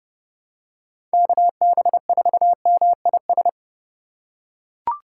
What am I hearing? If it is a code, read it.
Morse code, heard as a keyed tone, about 30 wpm, 700 Hz. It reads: KB4MIS
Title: USA Call Signs - 3x Character Spacing - 30wpm